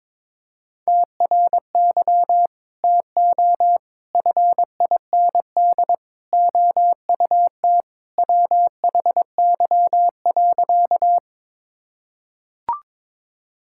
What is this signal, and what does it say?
Morse code, heard as a keyed tone, about 22 wpm, 700 Hz. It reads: TRY TO FIND OUT WHY.
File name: practice / 22wpm / Sentences with 3-Word Phrases - 22wpm